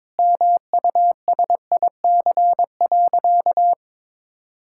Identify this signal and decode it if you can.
Morse code, heard as a keyed tone, about 22 wpm, 700 Hz. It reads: MUSIC.
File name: practice / 22wpm / Sentences with 3-Word Phrases - 22wpm